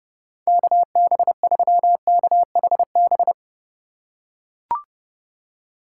Morse code, heard as a keyed tone, about 29 wpm, 700 Hz. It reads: KB3KHB